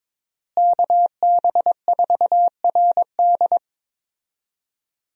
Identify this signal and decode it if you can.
Morse code, heard as a keyed tone, about 22 wpm, 700 Hz. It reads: KB4RD
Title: USA Call Signs - 5 Characters - Repeated 2x - 22wpm